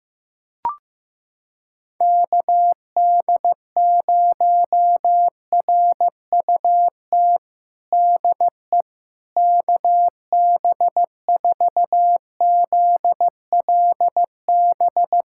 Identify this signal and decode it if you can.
Morse code, heard as a keyed tone, about 15 wpm, 700 Hz. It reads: KD0RUT DE KB4ZLB
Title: USA Callsign DE Callsign - 6 Characters - 15wpm